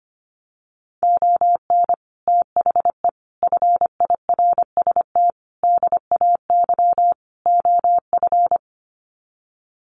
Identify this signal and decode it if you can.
Morse code, heard as a keyed tone, about 25 wpm, 700 Hz. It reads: ON THE FIRST DAY OF
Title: Sets of 5 Words Encore 25wpm